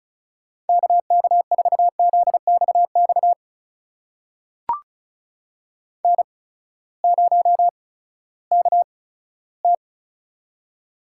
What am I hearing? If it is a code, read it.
Morse code, heard as a keyed tone, about 36 wpm, spaced as for 18 wpm, 700 Hz. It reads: KK4ZXX N0KT
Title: USA Call Signs - 8x Character Spacing - 35wpm